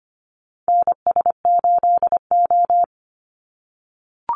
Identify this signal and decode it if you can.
Morse code, heard as a keyed tone, about 25 wpm, 700 Hz. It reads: NS8O
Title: Call Signs - 25wpm